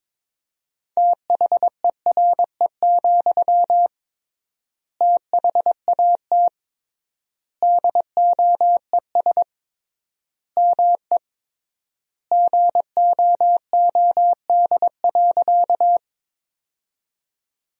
Morse code, heard as a keyed tone, about 21 wpm, 700 Hz. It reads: THERE, THAT DOES ME GOOD.
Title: 3x Word-Spacing - Sentences from Top 200 Words - 22wpm